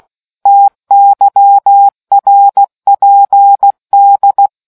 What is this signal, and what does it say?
Morse code, heard as a keyed tone, about 16 wpm, 800 Hz. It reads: TYRPD